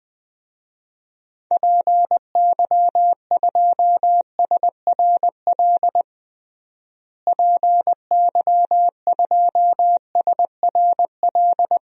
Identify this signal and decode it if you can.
Morse code, heard as a keyed tone, about 21 wpm, 700 Hz. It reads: PY2SRL PY2SRL
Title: DXCC-100 Call Signs - 6 Characters - Repeated 2x - 20wpm